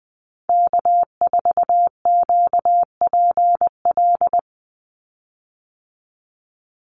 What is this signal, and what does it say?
Morse code, heard as a keyed tone, about 20 wpm, 700 Hz. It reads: K4QPL
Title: Call Signs Repeated 2x - 20wpm